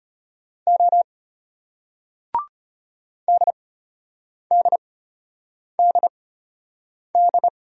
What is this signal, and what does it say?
Morse code, heard as a keyed tone, about 32 wpm, spaced as for 8 wpm, 700 Hz. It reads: O DDDD